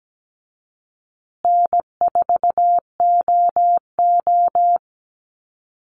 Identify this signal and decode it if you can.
Morse code, heard as a keyed tone, about 17 wpm, 700 Hz. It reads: N4OO
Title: Call Signs Repeated 2x - 17wpm